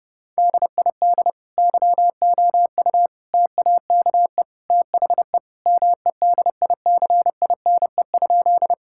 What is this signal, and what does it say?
Morse code, heard as a keyed tone, about 30 wpm, 700 Hz. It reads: DID YOU TAKE THE MEDICINE?
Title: Sentences from Top 1000 Words - Review Part 6 - 30wpm